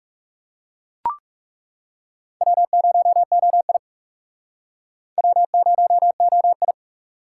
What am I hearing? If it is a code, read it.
Morse code, heard as a keyed tone, about 41 wpm, 700 Hz. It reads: W0OI W0OI